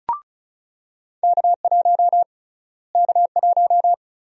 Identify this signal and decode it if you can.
Morse code, heard as a keyed tone, about 36 wpm, 700 Hz. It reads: K1 K1
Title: USA Call Sign Prefixes - Repeated 2x - 35wpm